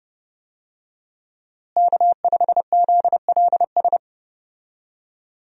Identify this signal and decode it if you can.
Morse code, heard as a keyed tone, about 30 wpm, 700 Hz. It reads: K5ZLS
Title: USA Call Signs - 5 Characters - Repeated 3x - 30wpm